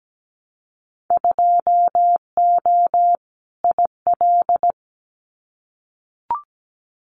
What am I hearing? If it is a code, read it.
Morse code, heard as a keyed tone, about 17 wpm, 700 Hz. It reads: 2O IL
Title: Winter Field Day Exchanges 17wpm